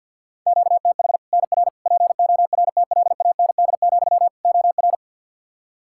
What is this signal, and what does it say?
Morse code, heard as a keyed tone, about 49 wpm, 700 Hz. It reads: QTH NR PORTLAND, OR
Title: Practice QSO - Exchange 1 - Beginning - 50wpm